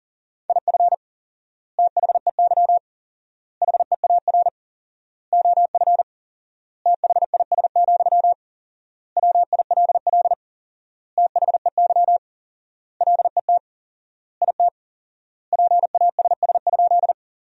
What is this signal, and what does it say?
Morse code, heard as a keyed tone, about 40 wpm, 700 Hz. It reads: IF THEY HEAR OF THIS, WILL THEY LET IT PASS?
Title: 4x Word-Spacing - Sentences from Top 400 Words - 40wpm